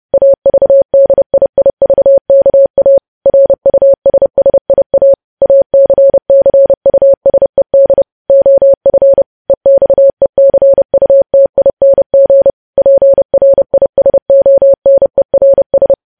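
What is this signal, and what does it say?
Morse code, heard as a keyed tone, about 30 wpm, 550 Hz. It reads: AVDIIVKA RUSSIA ACCUSED OF EXECUTING PRISONERS
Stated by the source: News headlines sent in Morse at 30.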